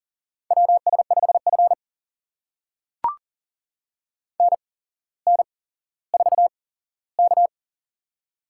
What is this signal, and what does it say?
Morse code, heard as a keyed tone, about 40 wpm, spaced as for 17 wpm, 690 Hz. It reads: WS5F NN4K
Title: USA Call Signs - 4 Characters - 8x Character Spacing - 40wpm